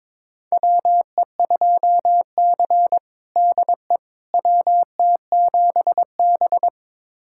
Morse code, heard as a keyed tone, about 21 wpm, 700 Hz. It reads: WE2C DE WT7B
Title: USA Callsign DE Callsign - 4 Characters - 22wpm